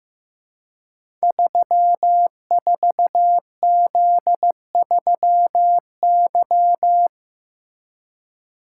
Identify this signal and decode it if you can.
Morse code, heard as a keyed tone, about 15 wpm, 700 Hz. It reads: 34Z3Y
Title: Code Groups - 5 Letter:Numbers - 15wpm